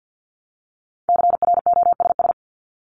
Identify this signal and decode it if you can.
Morse code, heard as a keyed tone, about 50 wpm, 700 Hz. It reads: CROSS
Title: Top 500 Words Repeated 50wpm